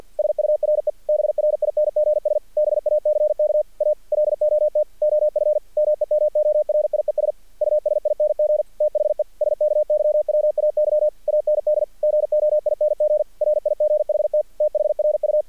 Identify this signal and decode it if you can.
Morse code, heard as a keyed tone, about 46 wpm, 610 Hz. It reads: SURE BRINGS BACK A LOT OF MEMORIES USING THE SKYWAY AND GOING RIGHT THRU
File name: archived / CRSnet